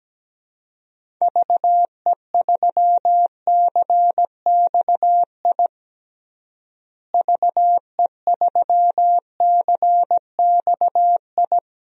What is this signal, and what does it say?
Morse code, heard as a keyed tone, about 17 wpm, 700 Hz. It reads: VE3CXI VE3CXI
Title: DXCC-100 Call Signs - 6 Characters - Repeated 3x - 17wpm